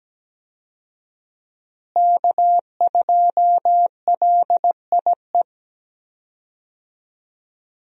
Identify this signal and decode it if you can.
Morse code, heard as a keyed tone, about 17 wpm, 700 Hz. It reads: K2LIE
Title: USA Call Signs - 5 Characters - Repeated 2x - 17wpm